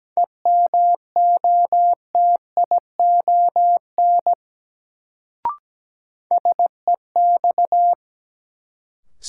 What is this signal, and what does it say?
Morse code, heard as a keyed tone, about 17 wpm, 700 Hz. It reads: EMOTION SEX
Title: Sentences from Top 1600 Words - New Word Drill - 17wpm